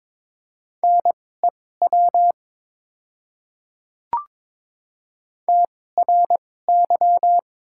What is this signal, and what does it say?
Morse code, heard as a keyed tone, about 22 wpm, spaced as for 16 wpm, 690 Hz. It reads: NEW TRY
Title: 2x Character Spacing - Top 200 Words - 22wpm